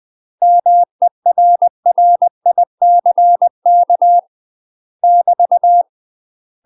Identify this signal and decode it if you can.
Morse code, heard as a keyed tone, about 20 wpm, 700 Hz. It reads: MERRICK <BT>
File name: UK MASTER QSOs TRAINING / 20WPM / UK MASTER QSOs 0026